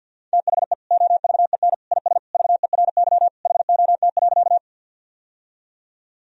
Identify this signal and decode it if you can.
Morse code, heard as a keyed tone, about 53 wpm, 700 Hz. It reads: THE OVEN IS VERY HOT.
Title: Sentences from Top 1000 Words - Review Part 4 - 50wpm